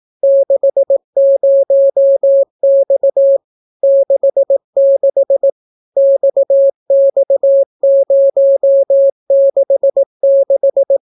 Morse code, heard as a keyed tone, about 18 wpm, 550 Hz. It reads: 60X 66 XX066